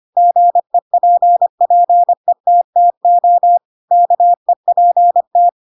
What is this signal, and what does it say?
Morse code, heard as a keyed tone, about 25 wpm, 700 Hz. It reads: GEPPETTO KEPT
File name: Pinocchio / Pinocchio ch3 25 25 ik1hge